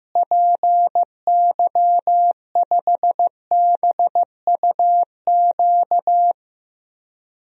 Morse code, heard as a keyed tone, about 15 wpm, 700 Hz. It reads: PY5BUQ
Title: DXCC-100 Call Signs - 6 Characters - Repeated 3x - 15wpm